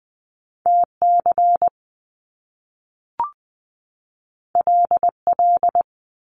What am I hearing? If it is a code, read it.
Morse code, heard as a keyed tone, about 20 wpm, 700 Hz. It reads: TC LL
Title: Top Two Letter Combinations in Words 20wpm